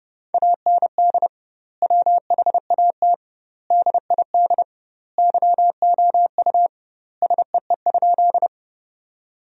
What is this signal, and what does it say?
Morse code, heard as a keyed tone, about 31 wpm, 700 Hz. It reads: AND WHAT DID YOU SEE?
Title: 2x Word-Spacing - Sentences from Top 100 Words - 30wpm